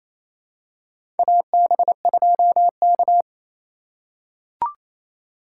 Morse code, heard as a keyed tone, about 28 wpm, 700 Hz. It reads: AB2K
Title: USA Call Signs - 4 Characters - 4x Character Spacing - 28wpm